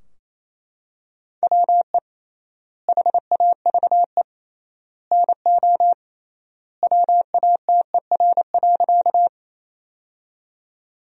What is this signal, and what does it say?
Morse code, heard as a keyed tone, about 28 wpm, 700 Hz. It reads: WE HAVE NO WATER.
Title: 3x Word-Spacing - Sentences from Top 100 Words - 28wpm